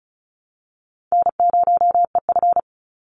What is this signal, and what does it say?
Morse code, heard as a keyed tone, about 34 wpm, 700 Hz. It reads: N0EF
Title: Call Signs Repeated 2x - 35wpm